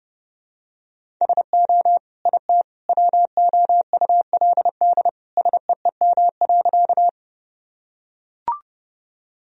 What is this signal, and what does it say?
Morse code, heard as a keyed tone, about 32 wpm, 700 Hz. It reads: SO IT WOULD SEEM.
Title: Sentences from Top 1000 Words - Review Part 1 - 30wpm